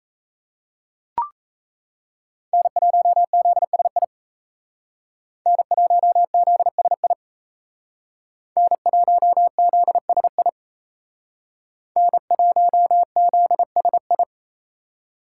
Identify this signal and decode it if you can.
Morse code, heard as a keyed tone, about 36 wpm, 700 Hz. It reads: N1ZSI N1ZSI N1ZSI N1ZSI